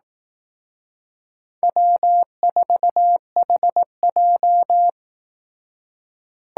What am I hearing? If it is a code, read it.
Morse code, heard as a keyed tone, about 18 wpm, 700 Hz. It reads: W4HJ
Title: Speed Racing - USA Call Signs 15wpm